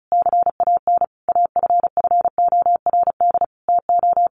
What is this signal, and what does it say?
Morse code, heard as a keyed tone, about 34 wpm, 700 Hz. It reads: CAN AFFORD TO
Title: Sentences from Top 1500 Words 35wpm